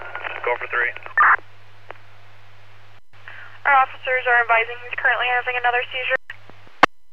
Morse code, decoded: none

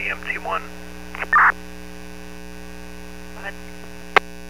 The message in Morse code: none

